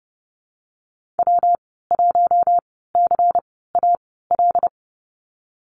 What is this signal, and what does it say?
Morse code, heard as a keyed tone, about 30 wpm, spaced as for 17 wpm, 700 Hz. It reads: W1CAL
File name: Lesson 19g - Letter w - Call Signs 30wpm